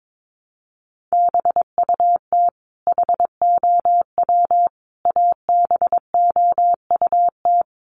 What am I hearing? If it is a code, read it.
Morse code, heard as a keyed tone, about 22 wpm, 700 Hz. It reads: BUT HOW ABOUT
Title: Sentences from Top 1300 Words 22wpm